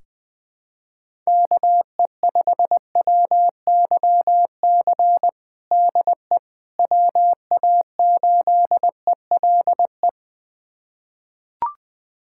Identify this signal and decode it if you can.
Morse code, heard as a keyed tone, about 20 wpm, 700 Hz. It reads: KE5WYC DE WA8ELE
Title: USA Callsign DE Callsign - 6 Characters - 20wpm